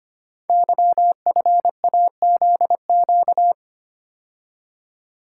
Morse code, heard as a keyed tone, about 25 wpm, 700 Hz. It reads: YFAZQ
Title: Code Groups - 5 Letter:Numbers - 25wpm